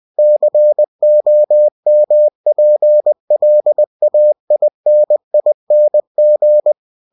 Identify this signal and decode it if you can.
Morse code, heard as a keyed tone, about 20 wpm, 600 Hz. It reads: COMPLAINING
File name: cw-books / winnie-the-pooh / winnie-20wpm-page-0003